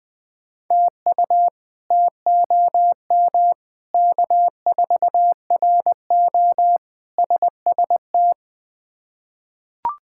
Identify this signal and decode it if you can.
Morse code, heard as a keyed tone, about 20 wpm, 700 Hz. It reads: TU TOM K4RO SST
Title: K1USN SST QSO Practice - 20wpm